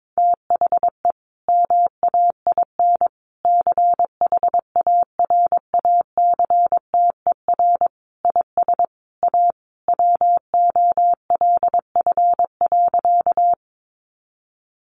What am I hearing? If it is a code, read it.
Morse code, heard as a keyed tone, about 22 wpm, 700 Hz. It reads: THE MAIN CHARACTER IS A WOLF.
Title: Additional Easy Phrases 22wpm